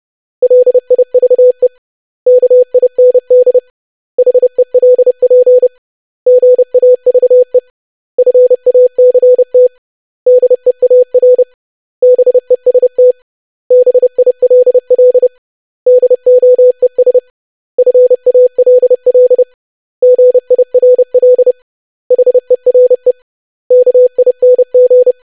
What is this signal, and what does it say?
Morse code, heard as a keyed tone, about 30 wpm, 500 Hz. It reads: LIVE KIND HELP GAVE FACT DEAR BEST BILL DOES FALL GIRL HERE KING
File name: lesson / web / basic500-25wpm